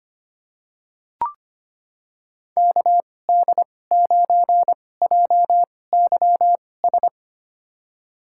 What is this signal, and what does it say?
Morse code, heard as a keyed tone, about 25 wpm, spaced as for 18 wpm, 700 Hz. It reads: KD9JYS
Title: USA Call Signs - 6 Characters - 2x Character Spacing - 25wpm